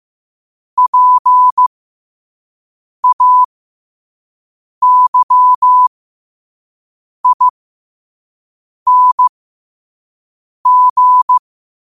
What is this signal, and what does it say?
Morse code, heard as a keyed tone, about 15 wpm, spaced as for 5 wpm, 1000 Hz. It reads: PAYING
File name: qotdfiles / 5wpm